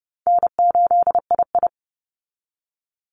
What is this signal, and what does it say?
Morse code, heard as a keyed tone, about 30 wpm, 700 Hz. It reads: N8II
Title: Call Signs - 30wpm